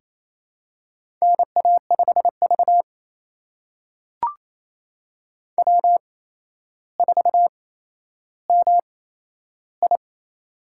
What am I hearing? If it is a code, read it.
Morse code, heard as a keyed tone, about 27 wpm, spaced as for 12 wpm, 700 Hz. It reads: NA5V W4MI
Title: USA Call Signs - 4 Characters - 8x Character Spacing - 28wpm